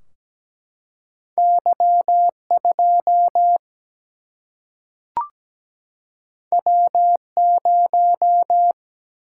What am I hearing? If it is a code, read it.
Morse code, heard as a keyed tone, about 17 wpm, 700 Hz. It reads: Y2 W0
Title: Single-Letter Single-Number 17wpm